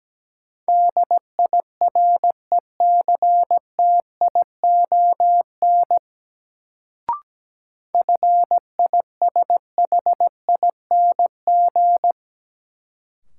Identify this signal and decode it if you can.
Morse code, heard as a keyed tone, about 17 wpm, 700 Hz. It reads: DIRECTION FISHING
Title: Sentences from Top 1000 Words - Review Part 5 - New Word Drill - 17wpm